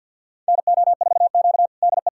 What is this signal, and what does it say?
Morse code, heard as a keyed tone, about 52 wpm, 700 Hz. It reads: NO4Q DE
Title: USA Callsign DE Callsign - 4 Characters - 50wpm